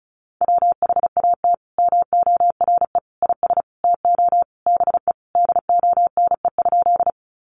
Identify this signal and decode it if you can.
Morse code, heard as a keyed tone, about 34 wpm, 700 Hz. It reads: WHAT MORE IS TO BE DONE?
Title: Sentences from Top 500 Words 35wpm